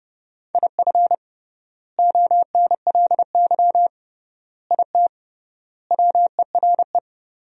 Morse code, heard as a keyed tone, about 28 wpm, 700 Hz. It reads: IF ONLY IT WERE
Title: 3x Word-Spacing - Sentences from Top 500 Words - 30wpm